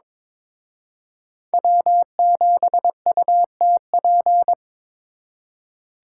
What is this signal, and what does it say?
Morse code, heard as a keyed tone, about 26 wpm, 700 Hz. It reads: W7UTP
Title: Speed Racing - USA 1x3 Call Signs 22wpm